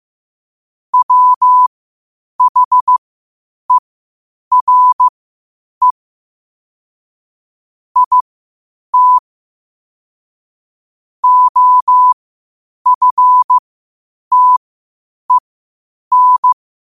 Morse code, heard as a keyed tone, about 15 wpm, spaced as for 8 wpm, 1000 Hz. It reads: WHERE IT OFTEN
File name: qotdfiles / 8wpm